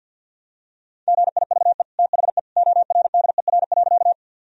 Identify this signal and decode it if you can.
Morse code, heard as a keyed tone, about 50 wpm, 700 Hz. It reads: GIVE THE ORDER.
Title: Sentences from Top 1000 Words - Review Part 2 - 50wpm